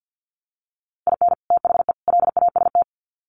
Morse code, heard as a keyed tone, about 48 wpm, 700 Hz. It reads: IN THE LAST
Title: Sets of 3 Words 50wpm